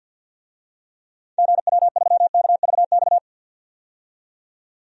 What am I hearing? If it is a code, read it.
Morse code, heard as a keyed tone, about 51 wpm, 700 Hz. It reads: GW3KVX
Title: DXCC-100 Call Signs - 6 Characters - 50wpm